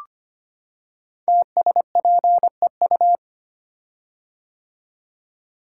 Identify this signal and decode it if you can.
Morse code, heard as a keyed tone, about 25 wpm, 700 Hz. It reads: TSPEU
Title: Code Groups - 5 Letter:Numbers - 25wpm